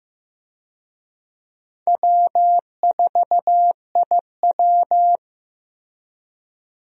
Morse code, heard as a keyed tone, about 15 wpm, 700 Hz. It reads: W4IW
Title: USA Call Signs - 4 Characters - Repeated 2x - 15wpm